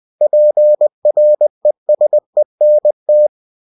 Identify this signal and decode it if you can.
Morse code, heard as a keyed tone, about 20 wpm, 600 Hz. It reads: PRESENT
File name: cw-books / winnie-the-pooh / winnie-20wpm-page-0107